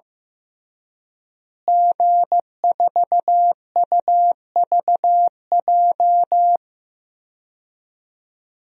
Morse code, heard as a keyed tone, about 15 wpm, 700 Hz. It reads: G4UVJ
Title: DXCC-100 Call Signs - 5 Characters - Repeated 3x - 15wpm